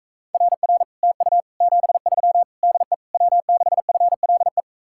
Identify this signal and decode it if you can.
Morse code, heard as a keyed tone, about 44 wpm, 700 Hz. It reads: RR TU 73 DE W6FLE